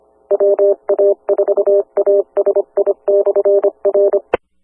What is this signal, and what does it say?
Morse code, heard as a keyed tone, about 25 wpm, 600 Hz, with noise. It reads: WA4ASI/R
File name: Skywarn/2013/2013-03-18-Newton County WA4ASI Repeater